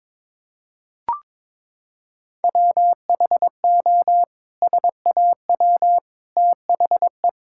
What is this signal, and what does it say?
Morse code, heard as a keyed tone, about 21 wpm, 700 Hz. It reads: WHO SAW THE
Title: Sentences from Top 1600 Words 22wpm